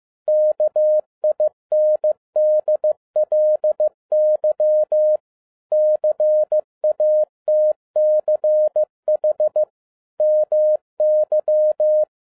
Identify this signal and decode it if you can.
Morse code, heard as a keyed tone, about 15 wpm, 610 Hz. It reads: KINDLY CATCH MY
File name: EXCITING QRP CW OPERATION 15wpm